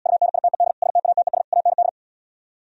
Morse code, heard as a keyed tone, about 77 wpm, 700 Hz. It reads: UNITED STATES AND